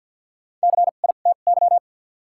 Morse code, heard as a keyed tone, about 54 wpm, spaced as for 37 wpm, 700 Hz. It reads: CITY
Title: 2x Character Spacing - Top 300 Words - 50wpm